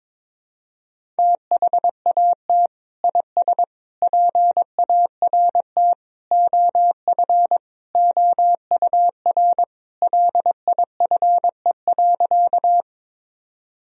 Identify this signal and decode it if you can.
Morse code, heard as a keyed tone, about 22 wpm, 700 Hz. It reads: THAT IS PART OF OUR LIFE.
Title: Sentences from Top 1000 Words - Review Part 1 - 22wpm